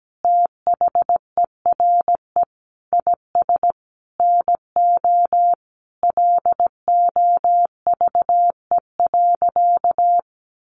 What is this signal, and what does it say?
Morse code, heard as a keyed tone, about 17 wpm, 700 Hz. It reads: THERE IS NO LOVE.
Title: Sentences from Top 1200 Words 17wpm